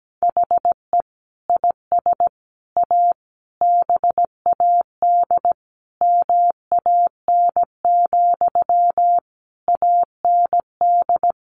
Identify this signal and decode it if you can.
Morse code, heard as a keyed tone, about 17 wpm, 700 Hz. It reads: HE IS A BAD MAN, AND
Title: Sentences from Top 1100 Words 17wpm